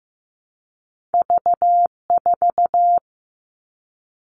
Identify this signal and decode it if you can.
Morse code, heard as a keyed tone, about 15 wpm, 700 Hz. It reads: V4
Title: All Cut Numbers - 2-Digit 15wpm